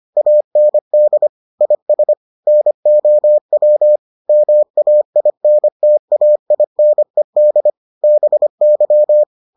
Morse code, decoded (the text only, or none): AND IS NOW MAINTAINED BY